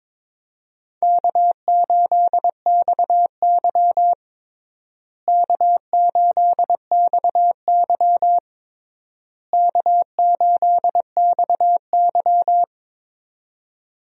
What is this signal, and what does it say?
Morse code, heard as a keyed tone, about 22 wpm, 700 Hz. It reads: K8XY K8XY K8XY
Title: USA Call Signs - 4 Characters - Repeated 3x - 22wpm